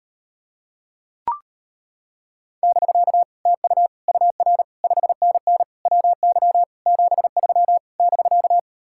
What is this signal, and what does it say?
Morse code, heard as a keyed tone, about 33 wpm, 700 Hz. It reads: <BK> TU UR 5NN WY 73 <BK>